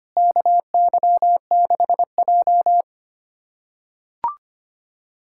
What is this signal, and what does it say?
Morse code, heard as a keyed tone, about 25 wpm, 700 Hz. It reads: KY6J
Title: USA Callsign DE Callsign - 4 Characters - 25wpm